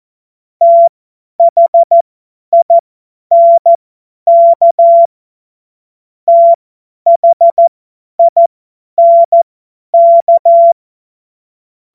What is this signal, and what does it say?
Morse code, heard as a keyed tone, about 14 wpm, spaced as for 10 wpm, 680 Hz. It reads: THINK THINK
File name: Morsekurs / Witze / Witz 14. Dezember 2021, 50 F70